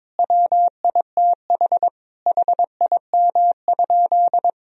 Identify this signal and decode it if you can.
Morse code, heard as a keyed tone, about 22 wpm, 700 Hz. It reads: WITH HIM?
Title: Sentences from Top 1000 Words - Review Part 3 - 22wpm